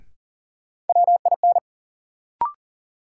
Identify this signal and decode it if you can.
Morse code, heard as a keyed tone, about 39 wpm, 700 Hz. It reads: WIN
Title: Sentences from Top 1500 Words - New Word Drill - 40wpm